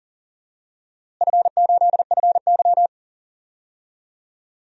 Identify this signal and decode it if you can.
Morse code, heard as a keyed tone, about 40 wpm, 700 Hz. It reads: F8FY
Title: DXCC-100 Call Signs - 4 Characters - Repeated 3x - 40wpm